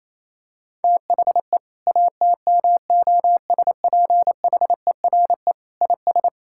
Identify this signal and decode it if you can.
Morse code, heard as a keyed tone, about 28 wpm, 700 Hz. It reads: THE ATMOSPHERE IS
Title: Sentences from Top 1600 Words 28wpm